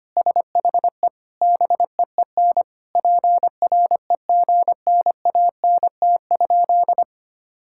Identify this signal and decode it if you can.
Morse code, heard as a keyed tone, about 25 wpm, 700 Hz. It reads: SHE BEEN PREGNANT?
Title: Sentences from Top 1000 Words - Review Part 6 - 25wpm